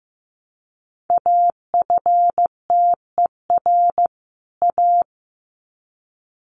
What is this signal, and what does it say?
Morse code, heard as a keyed tone, about 15 wpm, 700 Hz. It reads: AFTER A
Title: Sets of 2 Words 15wpm